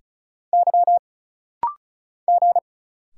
Morse code, heard as a keyed tone, about 36 wpm, 700 Hz. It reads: Y G
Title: Single Letters - Rapid-Fire 35wpm